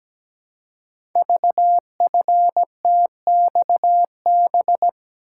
Morse code, heard as a keyed tone, about 17 wpm, 700 Hz. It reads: VFTXB
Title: Code Groups - 5 Letters - 17wpm